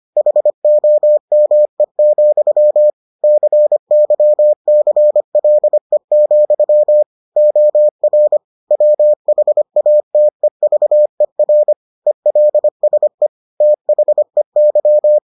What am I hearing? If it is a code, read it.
Morse code, heard as a keyed tone, about 25 wpm, 600 Hz. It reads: HOME, CYCLE, OR WHATEVER ELSE THEY